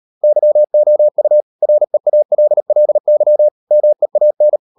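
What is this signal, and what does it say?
Morse code, heard as a keyed tone, about 38 wpm, 600 Hz. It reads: YOU REALLY MEAN
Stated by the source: Morse code practice at 35–39 wpm sending English text of Ken Follett's novel Never, from CW med Gnister.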